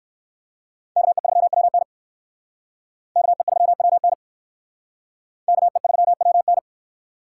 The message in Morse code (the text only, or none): KE3WN KE3WN KE3WN